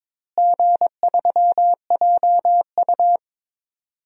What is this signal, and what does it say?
Morse code, heard as a keyed tone, about 22 wpm, 700 Hz. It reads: G3JU
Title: DXCC-100 Call Signs - 4 Characters - Repeated 3x - 22wpm